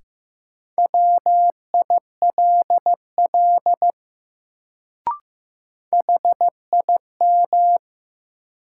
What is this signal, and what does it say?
Morse code, heard as a keyed tone, about 15 wpm, 700 Hz. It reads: WILL HIM
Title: Sets of 2 Words - New Word Drill - 15wpm